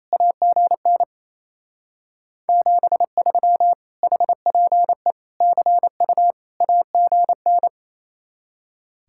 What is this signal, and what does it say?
Morse code, heard as a keyed tone, about 32 wpm, 700 Hz. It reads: AGN 73 HPE CU AGN
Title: Speed-Racing Common QSO Phrases - 28wpm